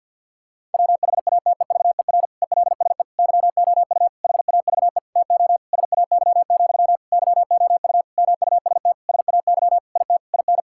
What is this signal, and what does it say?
Morse code, heard as a keyed tone, about 50 wpm, 700 Hz. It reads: WHATEVER ELSE YOU HAVE TO SAY, YOU MUST SAY IT IN
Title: Sentences from Top 1600 Words 50wpm